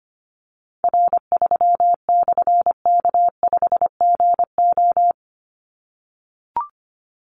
Morse code, heard as a keyed tone, about 25 wpm, 700 Hz. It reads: R3/K5GO
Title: Call Signs - 25wpm